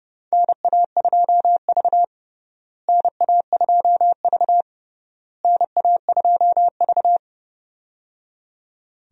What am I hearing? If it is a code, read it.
Morse code, heard as a keyed tone, about 30 wpm, 700 Hz. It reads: NA2V NA2V NA2V
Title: USA Call Signs - 4 Characters - Repeated 3x - 30wpm